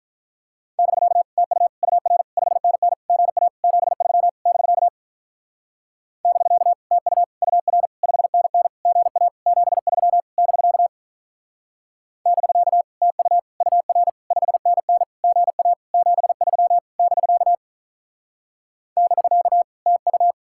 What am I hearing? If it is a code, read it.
Morse code, heard as a keyed tone, about 45 wpm, 700 Hz. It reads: <BK> TU UR 5NN GA 73 <BK> <BK> TU UR 5NN GA 73 <BK> <BK> TU UR 5NN GA 73 <BK> <BK> TU